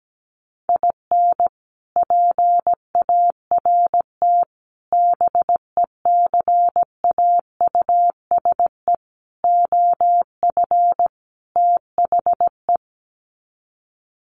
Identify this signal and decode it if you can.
Morse code, heard as a keyed tone, about 17 wpm, 700 Hz. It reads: IN PART BECAUSE OF THE
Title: Sets of 5 Words Encore2 17wpm